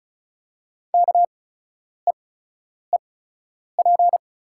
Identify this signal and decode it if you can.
Morse code, heard as a keyed tone, about 36 wpm, spaced as for 10 wpm, 690 Hz. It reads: KEEP
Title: 8x Character Spacing - Top 500 Words - 35wpm